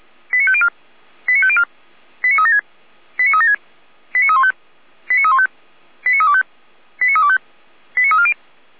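Five-tone selective call (ZVEI), heard as quick runs of five tones, tones of about 70 ms. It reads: 89504 89504 89376 89376 89315 89315 89325 89325 89340
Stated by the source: ZVEI five-tone selective call received with an SDR.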